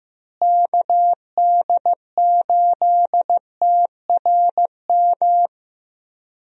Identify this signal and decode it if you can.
Morse code, heard as a keyed tone, about 15 wpm, 700 Hz. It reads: KD8TRM
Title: USA Call Signs - 6 Characters - Repeated 2x - 15wpm